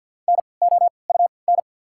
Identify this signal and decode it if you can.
Morse code, heard as a keyed tone, about 48 wpm, spaced as for 28 wpm, 690 Hz. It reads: NOUN
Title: 3x Character Spacing - Top 500 Words - 50wpm